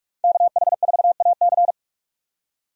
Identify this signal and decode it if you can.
Morse code, heard as a keyed tone, about 46 wpm, 700 Hz. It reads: KH4AC
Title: USA Call Signs - 5 Characters - 45wpm